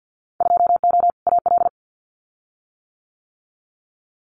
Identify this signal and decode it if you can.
Morse code, heard as a keyed tone, about 50 wpm, 700 Hz. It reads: 2O AL
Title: Winter Field Day Exchanges 50wpm